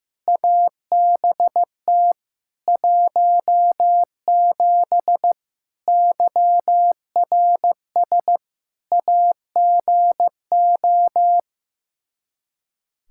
Morse code, heard as a keyed tone, about 15 wpm, 700 Hz. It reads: ABT 17 YRS AGO
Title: K1USN SST QSO Practice 15wpm